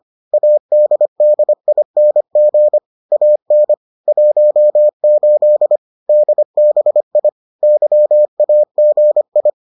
Morse code, heard as a keyed tone, about 25 wpm, 600 Hz. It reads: ADDING AN 18 DBI YAGI